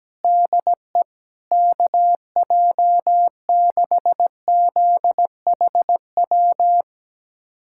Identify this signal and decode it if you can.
Morse code, heard as a keyed tone, about 17 wpm, 700 Hz. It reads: DE KJ6ZHW
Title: USA Callsign DE Callsign - 6 Characters - 17wpm